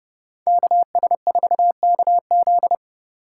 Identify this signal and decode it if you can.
Morse code, heard as a keyed tone, about 29 wpm, 700 Hz. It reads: KS4KZ
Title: USA Call Signs - 5 Characters - 8x Character Spacing - 30wpm